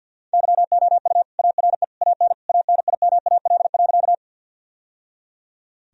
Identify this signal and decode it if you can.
Morse code, heard as a keyed tone, about 50 wpm, 700 Hz. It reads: YOU ARE AN ANIMAL.
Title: Sentences from Top 1000 Words - Review Part 1 - 50wpm